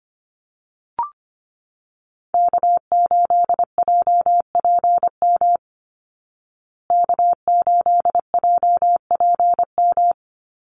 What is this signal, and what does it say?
Morse code, heard as a keyed tone, about 25 wpm, 700 Hz. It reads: K8JPM K8JPM
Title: Call Signs Repeated 2x - 25wpm